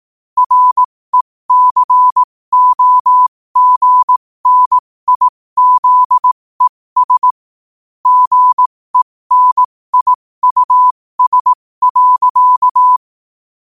Morse code, heard as a keyed tone, about 18 wpm, spaced as for 15 wpm, 1000 Hz. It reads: RECOGNIZES GENIUS.
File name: qotdfiles / 15wpm